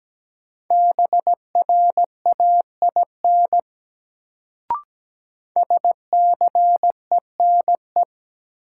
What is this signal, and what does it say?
Morse code, heard as a keyed tone, about 17 wpm, 700 Hz. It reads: BRAIN SCENE
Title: Sentences from Top 1100 Words - New Word Drill Alt - 17wpm